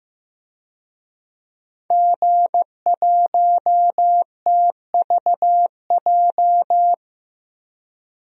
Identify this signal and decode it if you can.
Morse code, heard as a keyed tone, about 15 wpm, 700 Hz. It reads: G1TVJ
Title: DXCC-100 Call Signs - 5 Characters - 15wpm